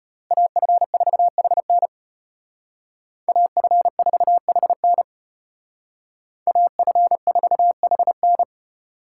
Morse code, heard as a keyed tone, about 31 wpm, 700 Hz. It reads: AF4HN AF4HN AF4HN
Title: Speed Racing - USA 2x2 Call Signs 25wpm